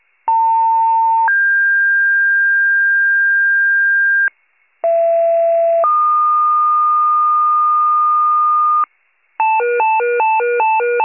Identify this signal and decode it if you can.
Two-tone page, heard as a one-second tone, then a three-second tone, about 900 then 1600 Hz, about 670 then 1220 Hz, with noise. It is heard from 0.3 to 8.8 s.